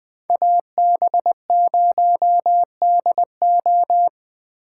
Morse code, heard as a keyed tone, about 23 wpm, 700 Hz. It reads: AB0DO
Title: Speed Racing - USA 2x2 Call Signs 20wpm